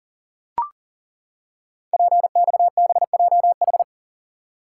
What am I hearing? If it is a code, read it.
Morse code, heard as a keyed tone, about 42 wpm, 700 Hz. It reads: PXBJH
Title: Code Groups - 5 Letters - 40wpm